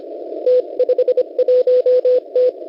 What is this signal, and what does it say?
Morse code, heard as a keyed tone, about 25 wpm, 500 Hz. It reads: T51T
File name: Unid- -0700utc-4735khz